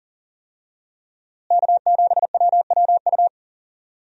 Morse code, heard as a keyed tone, about 40 wpm, 700 Hz. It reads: K7WWU
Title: USA Call Signs - 5 Characters - Repeated 2x - 40wpm